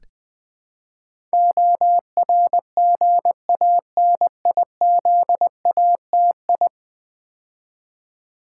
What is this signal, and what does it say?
Morse code, heard as a keyed tone, about 20 wpm, 700 Hz. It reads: ORGANIZATI
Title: Build A Word - 5 Letters At A Time - 12-Letter Words - 20wpm